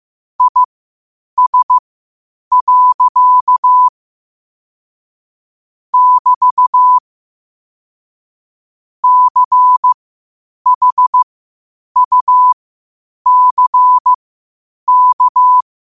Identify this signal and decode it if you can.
Morse code, heard as a keyed tone, about 15 wpm, spaced as for 8 wpm, 1000 Hz. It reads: IS. <BT> CHUCK
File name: qotdfiles / 8wpm